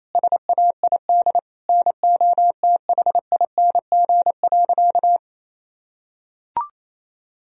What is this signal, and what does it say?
Morse code, heard as a keyed tone, about 28 wpm, 700 Hz. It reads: SAID NOTHING.
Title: Sentences from Top 1000 Words - Review Part 2 - 28wpm